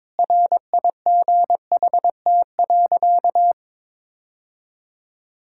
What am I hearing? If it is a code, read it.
Morse code, heard as a keyed tone, about 22 wpm, 700 Hz. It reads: RIGHT.
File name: practice / 22wpm / Sentences with 3-Word Phrases - 22wpm